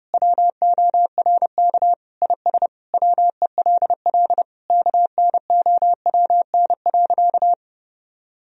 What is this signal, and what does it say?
Morse code, heard as a keyed tone, about 32 wpm, 700 Hz. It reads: WORK IS WELL KNOWN.
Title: Sentences from Top 1000 Words - Review Part 4 - 30wpm